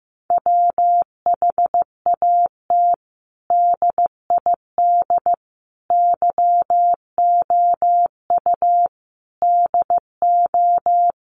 Morse code, heard as a keyed tone, about 15 wpm, 700 Hz. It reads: WHAT DID YOU DO
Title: Sentences from Top 100 Words 15wpm